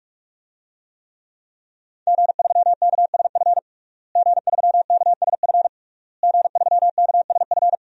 Morse code, heard as a keyed tone, about 41 wpm, 700 Hz. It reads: G3KSF G3KSF G3KSF
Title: DXCC-100 Call Signs - 5 Characters - Repeated 3x - 45wpm